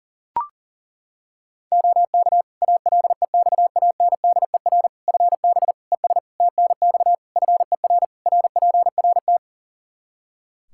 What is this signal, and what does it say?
Morse code, heard as a keyed tone, about 40 wpm, 700 Hz. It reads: OK ALEXANDER FB ES TNX FER RPRT
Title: K1USN SST QSO Practice 40wpm